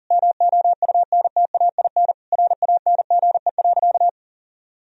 Morse code, heard as a keyed tone, about 40 wpm, 700 Hz. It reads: MOUNTAIN RANGE.
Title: Sentences from Top 1000 Words - Review Part 4 - 40wpm